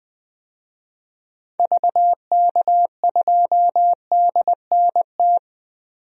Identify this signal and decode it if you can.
Morse code, heard as a keyed tone, about 20 wpm, 700 Hz. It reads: VK2DNT
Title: DXCC-100 Call Signs - 6 Characters - Repeated 3x - 20wpm